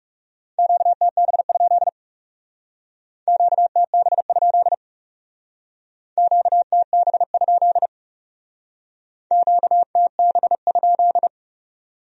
Morse code, heard as a keyed tone, about 40 wpm, 700 Hz. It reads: QTB? QTB? QTB? QTB?